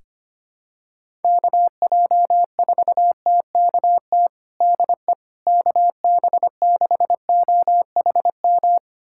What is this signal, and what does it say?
Morse code, heard as a keyed tone, about 25 wpm, 700 Hz. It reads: KJ4TKT DE KB6OHM
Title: USA Callsign DE Callsign 25wpm